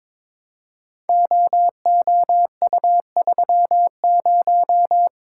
Morse code, heard as a keyed tone, about 23 wpm, 700 Hz. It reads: OOU30
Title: Code Groups - 5 Letter:Numbers - 22wpm